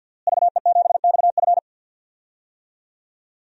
Morse code, heard as a keyed tone, about 51 wpm, 700 Hz. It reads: VE7XF
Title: DXCC-100 Call Signs - 5 Characters - Repeated 2x - 50wpm